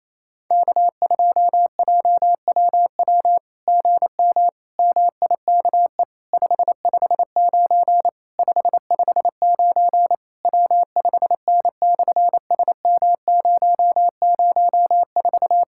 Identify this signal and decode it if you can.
Morse code, heard as a keyed tone, about 28 wpm, 700 Hz. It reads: K2JWW GM MIKE 559 559 W5N/SM004
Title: SOTA QSO Practice 28wpm